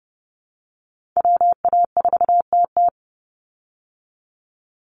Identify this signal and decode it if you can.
Morse code, heard as a keyed tone, about 30 wpm, 700 Hz. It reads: WA4TT
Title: Call Signs Repeated 3x - 30wpm